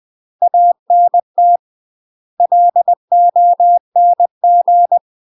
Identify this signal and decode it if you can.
Morse code, heard as a keyed tone, about 20 wpm, 700 Hz. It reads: ANT LONG
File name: UK MASTER QSOs TRAINING / 20WPM / UK MASTER QSOs 0069